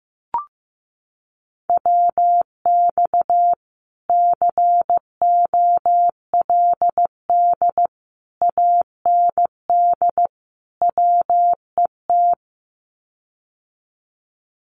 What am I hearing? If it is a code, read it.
Morse code, heard as a keyed tone, about 15 wpm, 700 Hz. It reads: WX COLD AND WET
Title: Common QSO Phrases Encore 15wpm